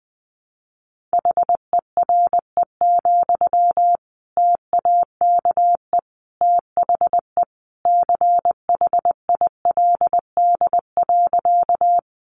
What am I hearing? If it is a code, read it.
Morse code, heard as a keyed tone, about 20 wpm, 700 Hz. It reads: HERE, TAKE THE CHILD.
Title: Sentences from Top 800 Words 20wpm